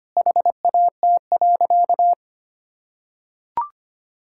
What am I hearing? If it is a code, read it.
Morse code, heard as a keyed tone, about 25 wpm, 700 Hz. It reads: HAT.
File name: Easy Sentences with Hints - 25wpm